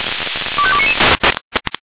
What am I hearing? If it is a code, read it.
Five-tone selective call (ZVEI), heard as quick runs of five tones, tones of about 70 ms. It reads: 25299
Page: /496/SDR/ZVEI/25237/